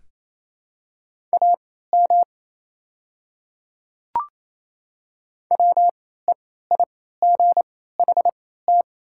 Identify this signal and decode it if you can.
Morse code, heard as a keyed tone, about 28 wpm, spaced as for 16 wpm, 700 Hz. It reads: AM WEIGHT
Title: 3x Character Spacing - Top 500 Words - 28wpm